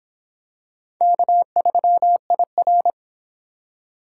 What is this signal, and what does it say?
Morse code, heard as a keyed tone, about 25 wpm, 700 Hz. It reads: K3IR